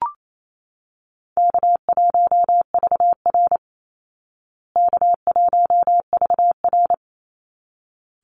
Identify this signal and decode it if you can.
Morse code, heard as a keyed tone, about 28 wpm, 700 Hz. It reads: K1VR K1VR
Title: Easy Call Signs Repeated 2x - 28wpm